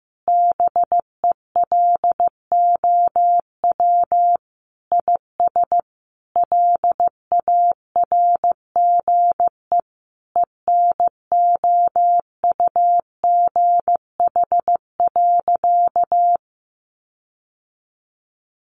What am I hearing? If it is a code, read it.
Morse code, heard as a keyed tone, about 15 wpm, 700 Hz. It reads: BELOW IS LARGE ENOUGH.
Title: Sentences from Top 1500 Words 15wpm